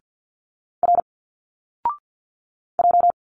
Morse code, heard as a keyed tone, about 50 wpm, 700 Hz. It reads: R J
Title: Single Letters 50wpm